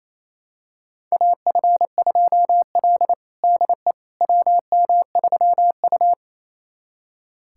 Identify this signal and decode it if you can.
Morse code, heard as a keyed tone, about 28 wpm, 700 Hz. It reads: AF2L DE WM3U
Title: USA Callsign DE Callsign - 4 Characters - 28wpm